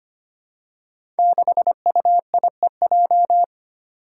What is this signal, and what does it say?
Morse code, heard as a keyed tone, about 25 wpm, 700 Hz. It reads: 6UIEJ
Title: Five Challenging Letter-Number Combinations 25wpm